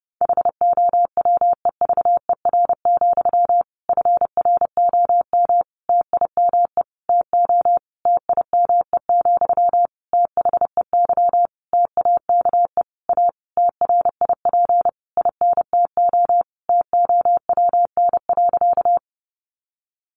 Morse code, heard as a keyed tone, about 30 wpm, 700 Hz. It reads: HOWEVER, FROM TIME TO TIME, THEY TAKE A TRIP INTO TOWN.
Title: Sentences from Top 1300 Words 30wpm